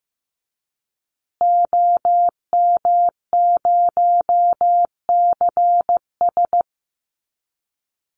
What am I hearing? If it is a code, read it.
Morse code, heard as a keyed tone, about 14 wpm, 700 Hz. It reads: OM0CS
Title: Call Signs Repeated 3x - 15wpm